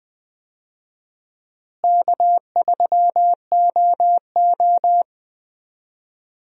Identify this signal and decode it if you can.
Morse code, heard as a keyed tone, about 20 wpm, 700 Hz. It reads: K3OO
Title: USA Call Signs - 4 Characters - Repeated 3x - 20wpm